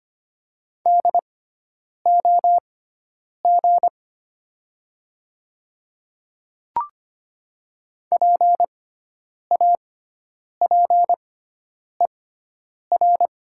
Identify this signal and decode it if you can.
Morse code, heard as a keyed tone, about 25 wpm, spaced as for 9 wpm, 700 Hz. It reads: DOG PAPER
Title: 6x Character Spacing - Top 500 Words - 25wpm